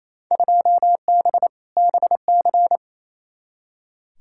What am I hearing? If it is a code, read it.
Morse code, heard as a keyed tone, about 28 wpm, 700 Hz. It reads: 2B BC
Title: Field Day Exchanges Encore - 28wpm